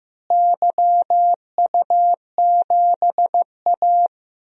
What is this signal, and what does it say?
Morse code, heard as a keyed tone, about 15 wpm, 700 Hz. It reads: YU7A
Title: DXCC-100 Call Signs - 4 Characters - Repeated 3x - 15wpm